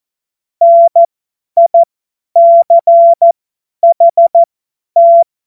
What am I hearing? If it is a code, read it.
Morse code, heard as a keyed tone, about 14 wpm, spaced as for 10 wpm, 680 Hz. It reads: NICHT
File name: Morsekurs / Witze / Witz 18. April 2023 50 F70